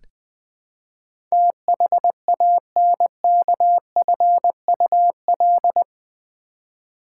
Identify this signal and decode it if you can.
Morse code, heard as a keyed tone, about 20 wpm, 700 Hz. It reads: THANKFUL
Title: Build A Word - 4 Letters At A Time - 10-Letter Words - 20wpm